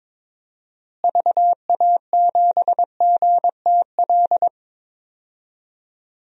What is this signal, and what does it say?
Morse code, heard as a keyed tone, about 22 wpm, 700 Hz. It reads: VA7GTL
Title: DXCC-100 Call Signs - 6 Characters - Repeated 3x - 22wpm